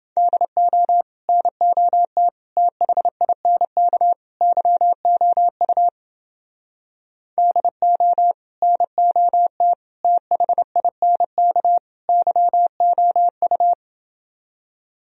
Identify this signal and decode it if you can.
Morse code, heard as a keyed tone, about 30 wpm, 700 Hz. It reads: DO NOT THINK YOU DO NOT THINK YOU